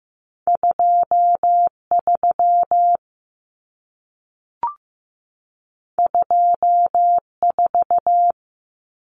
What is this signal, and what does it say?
Morse code, heard as a keyed tone, about 15 wpm, 700 Hz. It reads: 23 24
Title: Common Cut Numbers - 2-Digit 15wpm